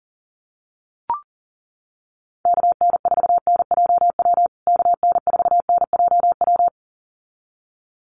Morse code, heard as a keyed tone, about 40 wpm, 700 Hz. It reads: KN4NJW KN4NJW
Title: USA Call Signs Repeated 2x 40wpm